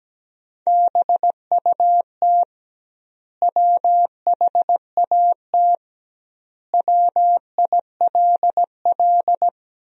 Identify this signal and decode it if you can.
Morse code, heard as a keyed tone, about 17 wpm, 700 Hz. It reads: BUT WHAT WILL
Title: 2x Word-Spacing - Sentences from Top 100 Words - 17wpm